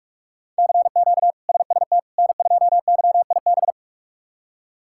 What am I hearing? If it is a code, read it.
Morse code, heard as a keyed tone, about 46 wpm, 700 Hz. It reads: CQ SST N2YIB
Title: K1USN SST QSO Practice - 45wpm